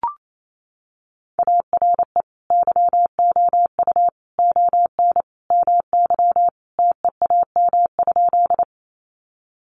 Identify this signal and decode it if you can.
Morse code, heard as a keyed tone, about 28 wpm, 700 Hz. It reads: ARE YOU ON MY TEAM?
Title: Sentences from Top 1500 Words 28wpm